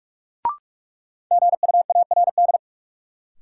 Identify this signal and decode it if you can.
Morse code, heard as a keyed tone, about 46 wpm, 700 Hz. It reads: GUARD
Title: Sentences from Top 700 Words - New Word Drill - 45wpm